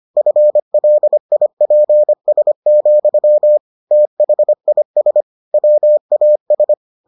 Morse code, heard as a keyed tone, about 25 wpm, 600 Hz. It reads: FLIPS, THIS WAS